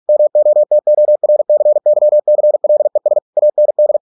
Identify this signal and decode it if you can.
Morse code, heard as a keyed tone, about 46 wpm, 600 Hz. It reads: MOTORCYCLES AND